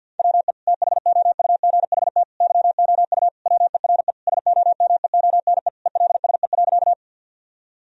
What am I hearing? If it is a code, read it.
Morse code, heard as a keyed tone, about 50 wpm, 700 Hz. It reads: WE THOUGHT YOU WERE SOMEONE ELSE.
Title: Sentences from Top 1000 Words - Review Part 4 - 50wpm